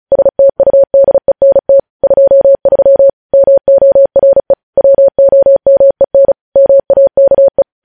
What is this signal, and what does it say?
Morse code, heard as a keyed tone, about 36 wpm, 550 Hz. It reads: STUDENT 23 MORE WOMEN MAKE
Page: news headlines in morse code at 35 wpm